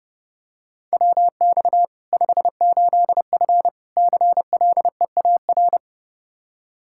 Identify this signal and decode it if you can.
Morse code, heard as a keyed tone, about 30 wpm, 700 Hz. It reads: WX 58F CLEAR
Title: Practice QSO - Exchange 2 - Info - 30wpm